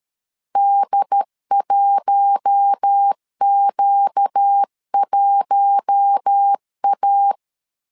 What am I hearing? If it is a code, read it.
Morse code, heard as a keyed tone, about 13 wpm, 800 Hz. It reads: D1Q1A
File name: LEZIONI / 4 Q1 12wpm 2028